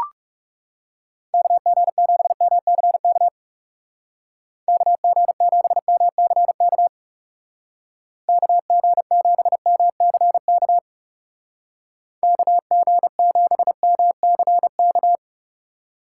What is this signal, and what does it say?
Morse code, heard as a keyed tone, about 35 wpm, 700 Hz. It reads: KG7MCK KG7MCK KG7MCK KG7MCK
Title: Speed Racing - USA 2x3 Call Signs 30wpm